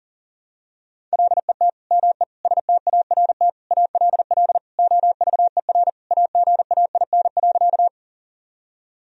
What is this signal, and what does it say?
Morse code, heard as a keyed tone, about 40 wpm, 700 Hz. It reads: LET ME START ALL OVER AGAIN.
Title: Sentences from Top 1000 Words - Review Part 1 - 40wpm